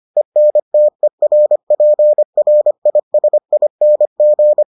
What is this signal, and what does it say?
Morse code, heard as a keyed tone, about 25 wpm, 600 Hz. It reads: ENTERPRISING